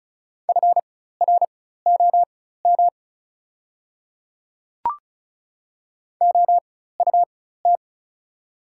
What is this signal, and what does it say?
Morse code, heard as a keyed tone, about 34 wpm, spaced as for 16 wpm, 700 Hz. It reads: FROM OUT
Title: 4x Character Spacing - Top 200 Words - 35wpm